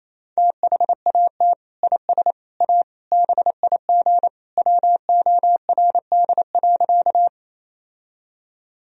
Morse code, heard as a keyed tone, about 28 wpm, 700 Hz. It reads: THAT IS A BIG WORD.
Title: Sentences from Top 1000 Words - Review Part 1 - 28wpm